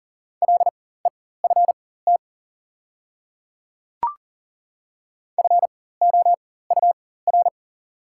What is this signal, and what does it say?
Morse code, heard as a keyed tone, about 40 wpm, spaced as for 19 wpm, 690 Hz. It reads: LEFT FOUR
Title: 4x Character Spacing - Top 300 Words - 40wpm